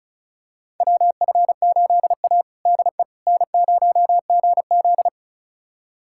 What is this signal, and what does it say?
Morse code, heard as a keyed tone, about 34 wpm, 700 Hz. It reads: WF8A DE N0GZ
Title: USA Callsign DE Callsign - 4 Characters - 35wpm